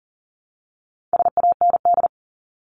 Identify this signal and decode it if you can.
Morse code, heard as a keyed tone, about 40 wpm, 700 Hz. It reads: SAND
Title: Top 1501-1750 Words 40wpm